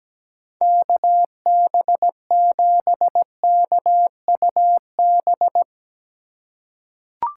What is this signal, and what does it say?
Morse code, heard as a keyed tone, about 17 wpm, 700 Hz. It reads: KB7KUB
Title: USA Call Signs - 4x Character Spacing - 17wpm